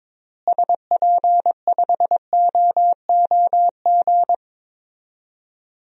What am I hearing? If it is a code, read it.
Morse code, heard as a keyed tone, about 22 wpm, 700 Hz. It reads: SP5OOG
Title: DXCC-100 Call Signs - 6 Characters - Repeated 3x - 22wpm